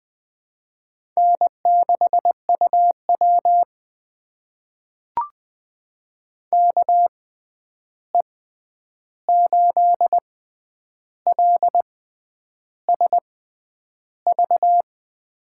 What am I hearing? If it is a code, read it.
Morse code, heard as a keyed tone, about 20 wpm, spaced as for 9 wpm, 700 Hz. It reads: N6UW KE8LSV
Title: USA Call Signs - 6x Character Spacing - 20wpm